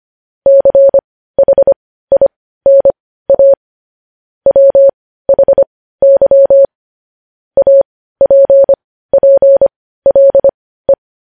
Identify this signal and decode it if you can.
Morse code, heard as a keyed tone, about 25 wpm, spaced as for 15 wpm, 550 Hz. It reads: CHINA WHY APPLE
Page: news headlines in morse code at 15 wpm